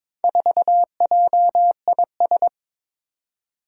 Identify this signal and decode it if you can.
Morse code, heard as a keyed tone, about 22 wpm, 700 Hz. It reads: 4JIS
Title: Four Challenging Letter-Number Combinations 22wpm